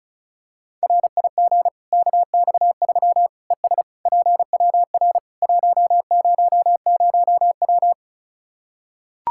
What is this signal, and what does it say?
Morse code, heard as a keyed tone, about 34 wpm, 700 Hz. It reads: RIG KX3 ES PWR 100W
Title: Practice QSO - Exchange 2 - Info - 35wpm